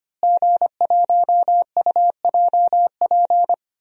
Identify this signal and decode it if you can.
Morse code, heard as a keyed tone, about 25 wpm, 700 Hz. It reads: G1UJP
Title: DXCC-100 Call Signs - 5 Characters - Repeated 3x - 25wpm